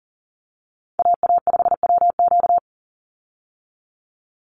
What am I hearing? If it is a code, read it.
Morse code, heard as a keyed tone, about 40 wpm, 700 Hz. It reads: AA5WQ
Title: USA Call Signs 40wpm